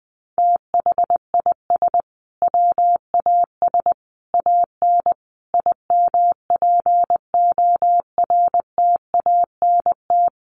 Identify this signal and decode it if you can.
Morse code, heard as a keyed tone, about 19 wpm, 700 Hz. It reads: THIS WAS AN IMPORTANT
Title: Sentences from Top 1300 Words 20wpm